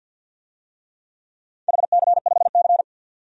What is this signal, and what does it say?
Morse code, heard as a keyed tone, about 51 wpm, 700 Hz. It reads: HC5C
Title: DXCC-100 Call Signs - 4 Characters - 50wpm